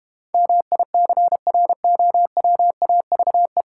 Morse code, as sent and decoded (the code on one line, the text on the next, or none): -- .. -.-. .-. --- .-- .- ...- .
MICROWAVE